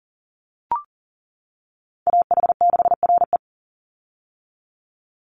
Morse code, heard as a keyed tone, about 40 wpm, 700 Hz. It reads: AH6RE
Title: Call Signs - 40wpm